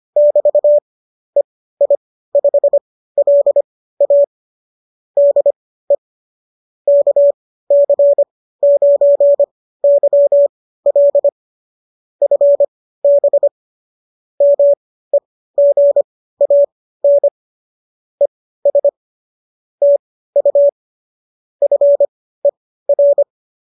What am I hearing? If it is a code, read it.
Morse code, heard as a keyed tone, about 25 wpm, spaced as for 15 wpm, 570 Hz. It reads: <BT> EI5LA DE KC9YL FB MEGAN ES TU FER